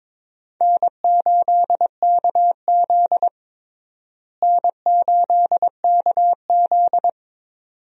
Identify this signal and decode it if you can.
Morse code, heard as a keyed tone, about 22 wpm, 700 Hz. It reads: N8KZ N8KZ
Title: USA Call Signs - 4 Characters - Repeated 2x - 22wpm